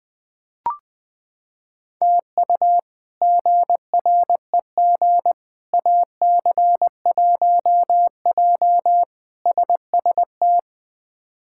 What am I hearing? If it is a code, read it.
Morse code, heard as a keyed tone, about 20 wpm, 700 Hz. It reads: TU GREG AC1J SST
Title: K1USN SST QSO Practice - 20wpm